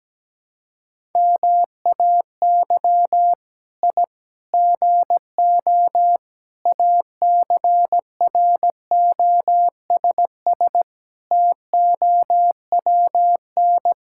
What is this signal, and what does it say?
Morse code, heard as a keyed tone, about 17 wpm, 700 Hz. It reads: MAY I GO ACROSS TOWN
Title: Sentences from Top 1000 Words - Review Part 5 - 17wpm